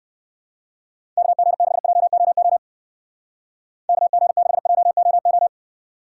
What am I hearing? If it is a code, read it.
Morse code, heard as a keyed tone, about 59 wpm, 700 Hz. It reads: KG6JOO KG6JOO